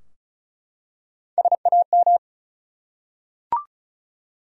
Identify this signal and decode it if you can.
Morse code, heard as a keyed tone, about 36 wpm, 700 Hz. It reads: SAM
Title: USA Common Names - Max 4 Letters - 35wpm